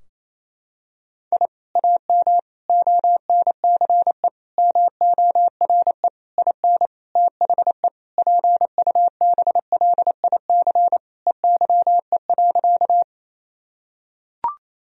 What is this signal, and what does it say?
Morse code, heard as a keyed tone, about 28 wpm, 700 Hz. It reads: I AM ONCE MORE IN THE PUBLIC EYE.
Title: Sentences from Top 1000 Words - Review Part 3 - 28wpm